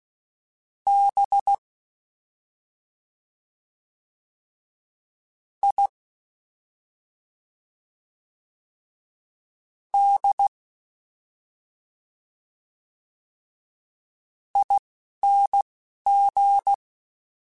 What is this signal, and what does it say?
Morse code, heard as a keyed tone, about 16 wpm, spaced as for 3 wpm, 800 Hz. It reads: BIDING
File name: Special INGs-mp3s / Special INGs-mp3s / Slow ing @10